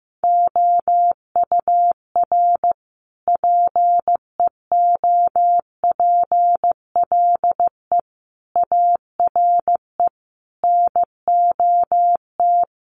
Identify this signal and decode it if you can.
Morse code, heard as a keyed tone, about 15 wpm, 700 Hz. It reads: OUR PEOPLE ARE NOT
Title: Sentences from Top 1500 Words 15wpm